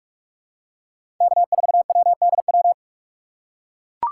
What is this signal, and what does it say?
Morse code, heard as a keyed tone, about 44 wpm, 700 Hz. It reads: K4WDW
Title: USA Call Signs - 4x Character Spacing - 45wpm